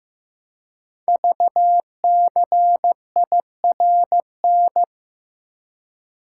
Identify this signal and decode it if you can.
Morse code, heard as a keyed tone, about 15 wpm, 700 Hz. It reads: VCIRN
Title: Code Groups - 5 Letter:Numbers - 15wpm